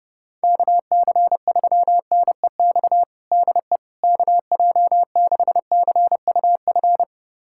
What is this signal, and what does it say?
Morse code, heard as a keyed tone, about 30 wpm, 700 Hz. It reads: KC3NEX DE KJ6CUF
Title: USA Callsign DE Callsign - 6 Characters - 30wpm